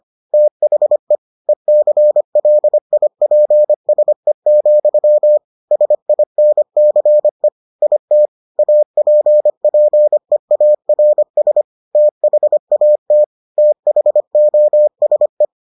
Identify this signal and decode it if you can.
Morse code, heard as a keyed tone, about 25 wpm, 600 Hz. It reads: THE ECLIPSE, SINCE IT APPEARS THAT THOSE